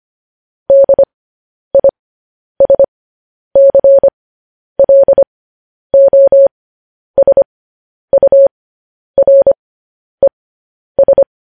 Morse code, heard as a keyed tone, about 25 wpm, spaced as for 10 wpm, 550 Hz. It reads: DISCLOSURES
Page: news headlines in morse code at 10 wpm